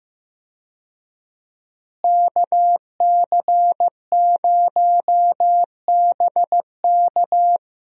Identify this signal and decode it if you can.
Morse code, heard as a keyed tone, about 15 wpm, 700 Hz. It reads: KC0BK
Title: USA Call Signs - 5 Characters - Repeated 3x - 15wpm